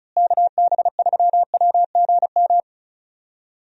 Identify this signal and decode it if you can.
Morse code, heard as a keyed tone, about 34 wpm, 700 Hz. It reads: KB3WGM
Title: USA Call Signs - 6 Characters - Repeated 3x - 35wpm